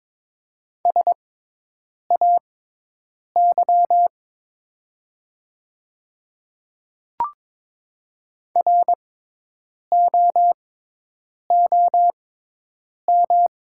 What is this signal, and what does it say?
Morse code, heard as a keyed tone, about 22 wpm, spaced as for 8 wpm, 700 Hz. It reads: SAY ROOM
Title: 6x Character Spacing - Top 300 Words - 22wpm